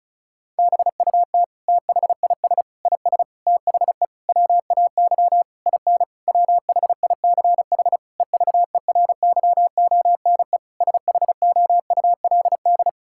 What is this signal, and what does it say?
Morse code, heard as a keyed tone, about 36 wpm, 700 Hz. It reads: BUT THIS IS THE WAY IN WHICH EVERYONE SHOULD
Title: Sentences from Top 1000 Words - Review Part 5 - 35wpm